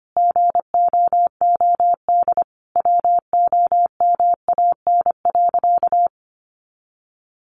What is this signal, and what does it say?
Morse code, heard as a keyed tone, about 25 wpm, 700 Hz. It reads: GOOD WOMAN.
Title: Sentences from Top 1100 Words 25wpm